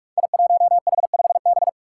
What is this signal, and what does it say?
Morse code, heard as a keyed tone, about 42 wpm, 700 Hz. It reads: I1H5B
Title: Five Challenging Letter-Number Combinations 45wpm